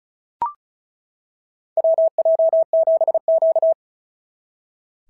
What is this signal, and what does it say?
Morse code, heard as a keyed tone, about 36 wpm, 690 Hz. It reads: WJ7Q
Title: SOTA QSO Practice 35wpm